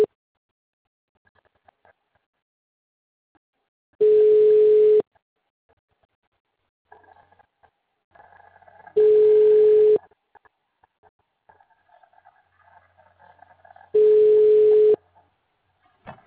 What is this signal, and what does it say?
Ringback tone (Europe, 425 Hz).
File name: out- -2047- - - .349